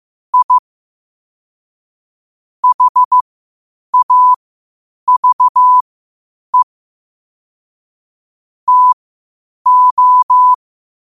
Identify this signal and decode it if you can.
Morse code, heard as a keyed tone, about 15 wpm, spaced as for 8 wpm, 1000 Hz. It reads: I HAVE TO
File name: qotdfiles / 8wpm